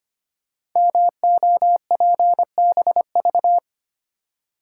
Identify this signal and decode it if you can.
Morse code, heard as a keyed tone, about 25 wpm, 700 Hz. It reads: MOPBV
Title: Code Groups - 5 Letters - 25wpm